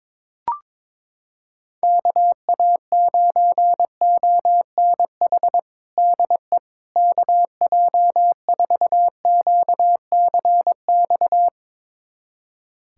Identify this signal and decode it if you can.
Morse code, heard as a keyed tone, about 22 wpm, 700 Hz. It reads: KA9ONH DE KJ4QCX
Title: USA Callsign DE Callsign - 6 Characters - 22wpm